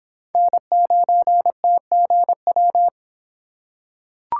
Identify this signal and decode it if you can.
Morse code, heard as a keyed tone, about 23 wpm, 700 Hz. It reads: N9TGW